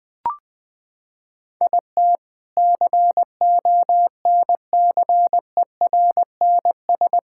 Morse code, heard as a keyed tone, about 20 wpm, 700 Hz. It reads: IT CONCERNS
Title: Sentences with 3-Word Phrases - 20wpm